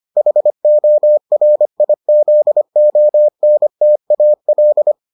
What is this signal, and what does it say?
Morse code, heard as a keyed tone, about 25 wpm, 600 Hz. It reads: HORIZONTAL